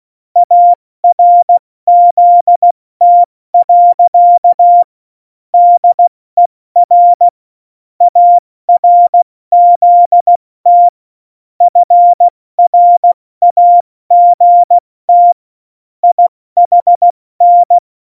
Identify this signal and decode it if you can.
Morse code, heard as a keyed tone, about 16 wpm, 700 Hz. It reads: ARZT. DER ARZT FRAGT IHN